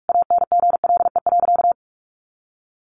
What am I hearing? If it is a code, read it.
Morse code, heard as a keyed tone, about 44 wpm, 700 Hz. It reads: ANGLE.
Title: Sentences from Top 1500 Words 45wpm